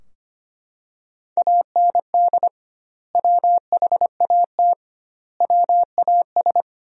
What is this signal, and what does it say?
Morse code, heard as a keyed tone, about 25 wpm, 700 Hz. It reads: AND WHAT WAS
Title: 2x Word-Spacing - Sentences from Top 100 Words - 25wpm